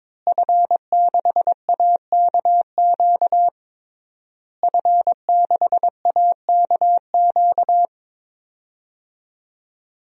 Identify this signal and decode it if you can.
Morse code, heard as a keyed tone, about 22 wpm, 700 Hz. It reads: F6AKQ F6AKQ
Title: DXCC-100 Call Signs - 5 Characters - Repeated 3x - 22wpm